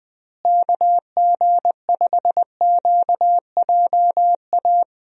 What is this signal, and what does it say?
Morse code, heard as a keyed tone, about 21 wpm, 700 Hz. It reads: KG5QJA
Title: USA Call Signs - 6 Characters - Repeated 3x - 20wpm